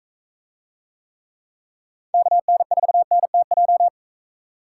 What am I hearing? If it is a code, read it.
Morse code, heard as a keyed tone, about 43 wpm, 700 Hz. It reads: KN4NTJ